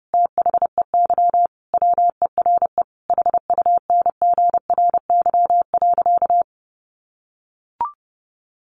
Morse code, heard as a keyed tone, about 30 wpm, 700 Hz. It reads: THEY WERE HUNGRY.
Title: Sentences from Top 1400 Words 30wpm